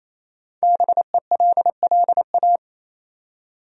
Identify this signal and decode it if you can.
Morse code, heard as a keyed tone, about 27 wpm, 700 Hz. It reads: BELLA
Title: USA Common Names - Max 5 Letters - 28wpm